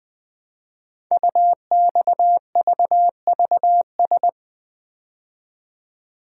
Morse code, heard as a keyed tone, about 20 wpm, 700 Hz. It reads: UXVVS
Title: Code Groups - 5 Letters - 20wpm